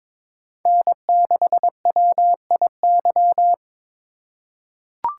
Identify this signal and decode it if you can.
Morse code, heard as a keyed tone, about 22 wpm, 700 Hz. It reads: N6WIY
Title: USA Call Signs - 5 Characters - 4x Character Spacing - 22wpm